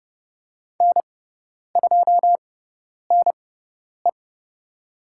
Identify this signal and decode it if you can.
Morse code, heard as a keyed tone, about 30 wpm, spaced as for 10 wpm, 700 Hz. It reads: N2NE
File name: Lesson 18g - Letter m - Call Signs 30wpm